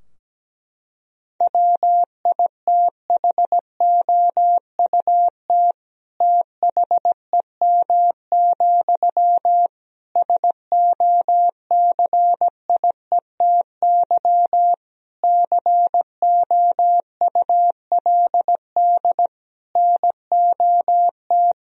Morse code, heard as a keyed tone, about 17 wpm, 700 Hz. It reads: WITHOUT THEM, SOCIETY COULD NOT
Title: Sentences from Top 1000 Words - Review Part 4 - 17wpm